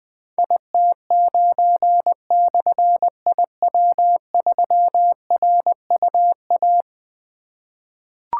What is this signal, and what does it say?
Morse code, heard as a keyed tone, about 20 wpm, 700 Hz. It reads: IT9/IW3RUA
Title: Difficult Call Signs 20wpm